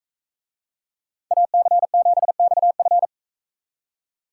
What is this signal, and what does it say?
Morse code, heard as a keyed tone, about 44 wpm, 700 Hz. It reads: AC7XF